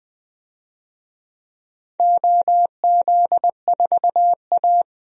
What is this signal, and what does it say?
Morse code, heard as a keyed tone, about 20 wpm, 700 Hz. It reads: OZ4A
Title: DXCC-100 Call Signs - 4 Characters - 20wpm